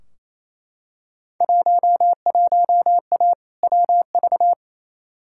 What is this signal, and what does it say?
Morse code, heard as a keyed tone, about 28 wpm, 700 Hz. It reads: 11A WV
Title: Field Day Exchanges Encore - 28wpm